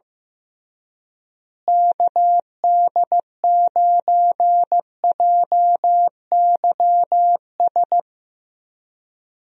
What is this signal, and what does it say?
Morse code, heard as a keyed tone, about 15 wpm, 700 Hz. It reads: KD9JYS
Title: USA Call Signs - 6 Characters - Repeated 3x - 15wpm